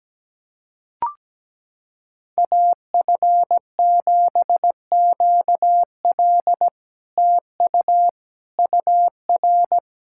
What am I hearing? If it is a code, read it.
Morse code, heard as a keyed tone, about 17 wpm, 700 Hz. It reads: AF7QL TU UR
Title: SOTA QSO Practice 17wpm